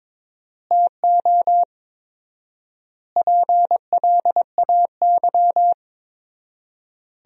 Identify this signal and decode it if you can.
Morse code, heard as a keyed tone, about 22 wpm, 700 Hz. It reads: TO PLAY
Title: 4x Word-Spacing - Sentences from Top 300 Words - 22wpm